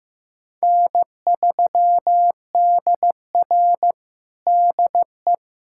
Morse code, heard as a keyed tone, about 16 wpm, 700 Hz. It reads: N3DR DE
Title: USA Callsign DE Callsign - 4 Characters - 15wpm